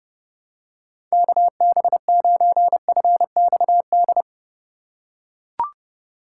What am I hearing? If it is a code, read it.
Morse code, heard as a keyed tone, about 30 wpm, 700 Hz. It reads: KB9FXD
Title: USA Call Signs - 6 Characters - 4x Character Spacing - 30wpm